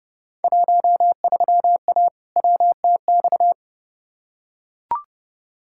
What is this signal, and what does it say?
Morse code, heard as a keyed tone, about 32 wpm, 700 Hz. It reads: 13A WTX
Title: Field Day Exchanges Encore - 30wpm